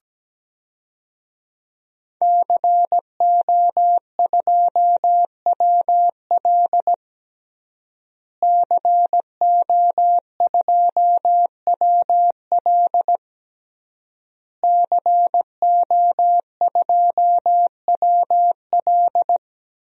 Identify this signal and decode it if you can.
Morse code, heard as a keyed tone, about 17 wpm, 700 Hz. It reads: CO2WL CO2WL CO2WL
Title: DXCC-100 Call Signs - 5 Characters - Repeated 3x - 17wpm